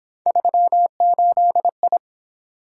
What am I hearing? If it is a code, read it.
Morse code, heard as a keyed tone, about 27 wpm, 700 Hz. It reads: 38I
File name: Speed-Racing - Three Letter-Number Combinations - 22wpm